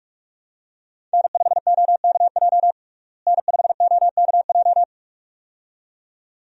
Morse code, heard as a keyed tone, about 43 wpm, 700 Hz. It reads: N5OKJ N5OKJ
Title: USA Call Signs - 5 Characters - Repeated 2x - 45wpm